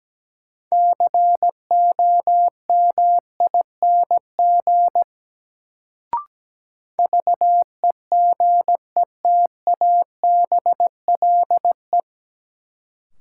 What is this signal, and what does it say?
Morse code, heard as a keyed tone, about 17 wpm, 700 Hz. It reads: COMING VEGETABLE
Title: Sentences from Top 1300 Words - New Word Drill Alt - 17wpm